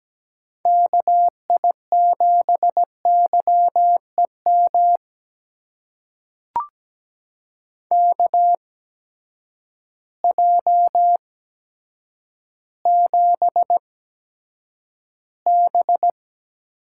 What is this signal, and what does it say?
Morse code, heard as a keyed tone, about 17 wpm, spaced as for 8 wpm, 700 Hz. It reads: KI7YEM KJ7B
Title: USA Call Signs - 6 Characters - 8x Character Spacing - 17wpm